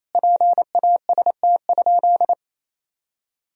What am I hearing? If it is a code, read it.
Morse code, heard as a keyed tone, about 28 wpm, 700 Hz. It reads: PAST?
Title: Sentences from Top 1000 Words - Review Part 4 - 28wpm